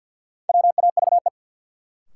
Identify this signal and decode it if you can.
Morse code, heard as a keyed tone, about 49 wpm, 700 Hz. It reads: WAVE
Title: Sentences from Top 1400 Words - New Word Drill Alt - 50wpm